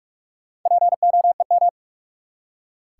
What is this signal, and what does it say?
Morse code, heard as a keyed tone, about 46 wpm, 700 Hz. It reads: POEM
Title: Four Letter Words - 45wpm